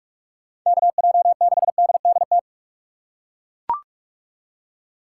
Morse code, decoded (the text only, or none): KJ6DDT